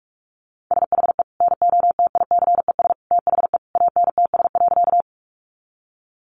Morse code, heard as a keyed tone, about 44 wpm, 700 Hz. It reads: SHE NOTICES THE ANTS.
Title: Additional Easy Phrases 45wpm